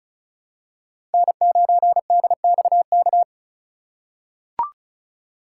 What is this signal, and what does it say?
Morse code, heard as a keyed tone, about 34 wpm, 700 Hz. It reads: N9DXK
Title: USA Call Signs - 5 Characters - 4x Character Spacing - 35wpm